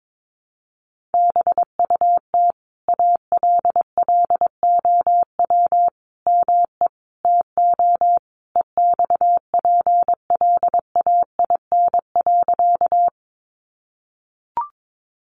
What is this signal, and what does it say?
Morse code, heard as a keyed tone, about 22 wpm, 700 Hz. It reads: BUT ALLOW ME TO EXPLAIN.
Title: Sentences from Top 1100 Words 22wpm